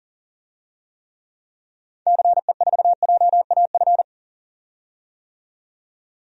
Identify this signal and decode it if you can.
Morse code, heard as a keyed tone, about 39 wpm, 700 Hz. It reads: CE4JAF
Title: DXCC-100 Call Signs - 6 Characters - 40wpm